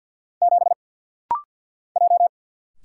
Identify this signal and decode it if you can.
Morse code, heard as a keyed tone, about 50 wpm, 690 Hz. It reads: 7 J
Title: Single Letter-Number Alt - Rapid-Fire 50wpm